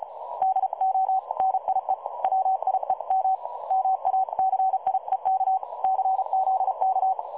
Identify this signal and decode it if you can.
Morse code, heard as a keyed tone, about 33 wpm, 760 Hz. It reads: DONIE WIEM MACIEK CZK
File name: dlazaawansowanych / nagraniazpasma / websdr recording 3565.0kHz